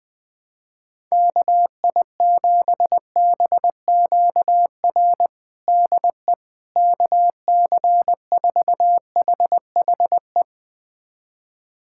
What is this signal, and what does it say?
Morse code, heard as a keyed tone, about 20 wpm, 700 Hz. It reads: KI7BQR DE KC4HHE
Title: USA Callsign DE Callsign 20wpm